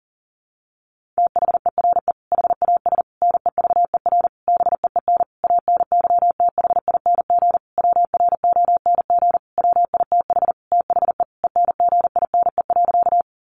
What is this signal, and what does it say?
Morse code, heard as a keyed tone, about 40 wpm, 700 Hz. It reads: THERE HAS NEVER BEEN ANYTHING WRONG WITH THE ENGINE.
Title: Sentences from Top 1500 Words 40wpm